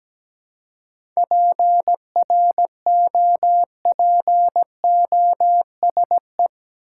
Morse code, heard as a keyed tone, about 17 wpm, 700 Hz. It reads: PROPOSE
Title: Seven Letter Words - 17wpm